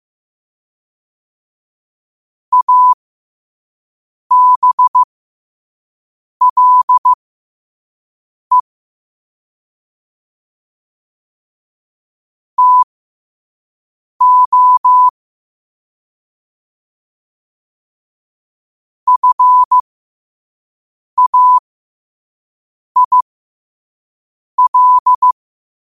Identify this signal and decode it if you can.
Morse code, heard as a keyed tone, about 15 wpm, spaced as for 5 wpm, 1000 Hz. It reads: ABLE TO FAIL
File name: qotdfiles / 5wpm